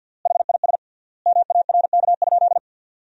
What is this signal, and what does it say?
Morse code, heard as a keyed tone, about 53 wpm, 700 Hz. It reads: HIS MARK?
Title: 3x Word-Spacing - Sentences from Top 300 Words - 50wpm